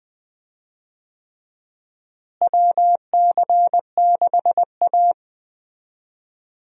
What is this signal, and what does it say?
Morse code, heard as a keyed tone, about 20 wpm, 700 Hz. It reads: WC6A
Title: USA Call Signs - 4 Characters - Repeated 3x - 20wpm